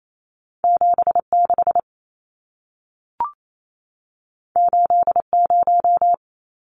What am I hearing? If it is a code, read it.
Morse code, heard as a keyed tone, about 28 wpm, 700 Hz. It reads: 76 80
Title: Numbers - 2-Digit 28wpm